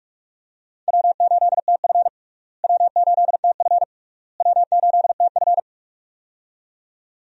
Morse code, heard as a keyed tone, about 46 wpm, 700 Hz. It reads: W8TF W8TF W8TF
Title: USA Call Signs - 4 Characters - Repeated 3x - 45wpm